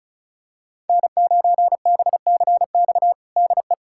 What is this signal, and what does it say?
Morse code, heard as a keyed tone, about 34 wpm, 700 Hz. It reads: N9BCX DE
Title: USA Callsign DE Callsign - 5 Characters - 35wpm